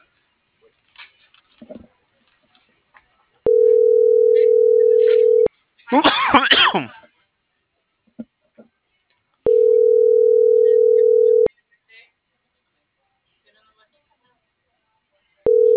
Ringback tone (North America), 440 + 480 Hz.